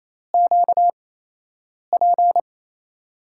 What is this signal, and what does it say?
Morse code, heard as a keyed tone, about 28 wpm, spaced as for 8 wpm, 700 Hz. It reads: QP
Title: USA Call Signs - 6 Characters - 8x Character Spacing - 28wpm